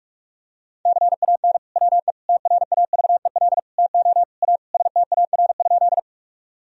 Morse code, heard as a keyed tone, about 42 wpm, 700 Hz. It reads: CAN WE TRAVEL TO A STAR?
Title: Sentences from Top 1000 Words - Review Part 2 - 45wpm